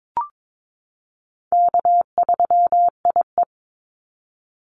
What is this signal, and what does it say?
Morse code, heard as a keyed tone, about 22 wpm, 700 Hz. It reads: K3IE
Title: Call Signs Repeated 3x - 22wpm